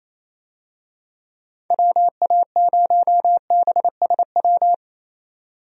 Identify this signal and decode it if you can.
Morse code, heard as a keyed tone, about 28 wpm, 700 Hz. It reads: WA0BSW
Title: USA Call Signs - 6 Characters - 28wpm